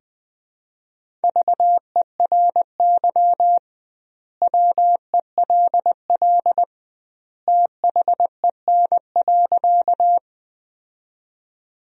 Morse code, heard as a keyed tone, about 20 wpm, 700 Hz. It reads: VERY WELL THEN.
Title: 2x Word-Spacing - Sentences from Top 200 Words - 20wpm